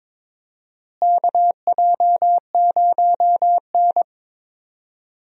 Morse code, heard as a keyed tone, about 22 wpm, 700 Hz. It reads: KJ0N
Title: USA Call Signs - 4 Characters - Repeated 3x - 22wpm